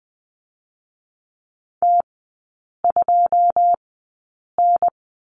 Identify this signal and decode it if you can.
Morse code, heard as a keyed tone, about 20 wpm, spaced as for 8 wpm, 700 Hz. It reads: T2N
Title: 081 Lesson 14e - Number 2 - Three Characters 20wpm